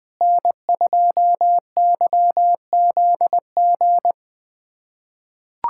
Morse code, heard as a keyed tone, about 21 wpm, 700 Hz. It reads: N2YZG
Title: USA Callsign DE Callsign - 5 Characters - 20wpm